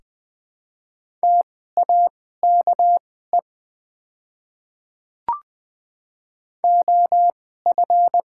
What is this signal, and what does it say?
Morse code, heard as a keyed tone, about 20 wpm, spaced as for 14 wpm, 700 Hz. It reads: TAKE OF
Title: 2x Character Spacing - Top 200 Words - 20wpm